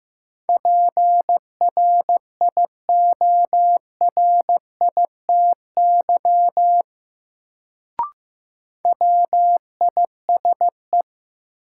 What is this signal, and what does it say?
Morse code, heard as a keyed tone, about 15 wpm, 700 Hz. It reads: PRIORITY WISE
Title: Sentences from Top 1600 Words - New Word Drill Alt - 15wpm